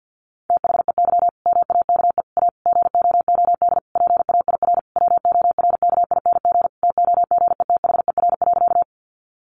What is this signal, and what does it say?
Morse code, heard as a keyed tone, about 50 wpm, 700 Hz. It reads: THEY MAKE A GOOD PAIR WORKING TOGETHER.
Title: Sentences from Top 1200 Words 50wpm